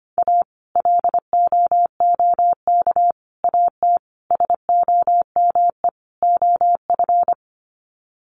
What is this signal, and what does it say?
Morse code, heard as a keyed tone, about 25 wpm, 700 Hz. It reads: A LOOK AT SOME OF
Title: Sets of 5 Words Encore2 25wpm